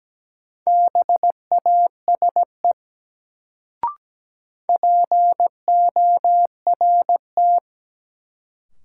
Morse code, heard as a keyed tone, about 17 wpm, 700 Hz. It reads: BASE PORT
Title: Sentences from Top 1000 Words - Review Part 1 - New Word Drill - 17wpm